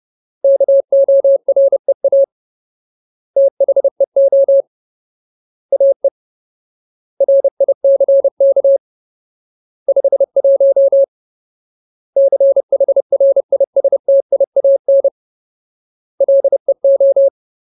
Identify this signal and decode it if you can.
Morse code, heard as a keyed tone, about 30 wpm, 550 Hz. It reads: KOREA THEO AE RICK 51 CHRISTIAN LEO
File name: Ultimate Word List / 30WPM Ultimate